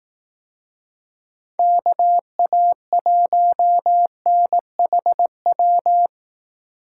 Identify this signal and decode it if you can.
Morse code, heard as a keyed tone, about 18 wpm, 700 Hz. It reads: KA1NHW
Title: Speed Racing - USA 2x3 Call Signs 15wpm